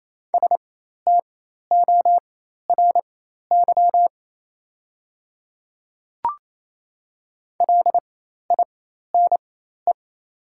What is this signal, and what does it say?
Morse code, heard as a keyed tone, about 28 wpm, spaced as for 13 wpm, 700 Hz. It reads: STORY LINE
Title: 4x Character Spacing - Top 500 Words - 28wpm